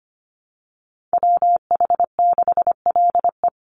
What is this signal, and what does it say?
Morse code, heard as a keyed tone, about 25 wpm, 700 Hz. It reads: WH6LE
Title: NA SOTA Activator Callsigns 25wpm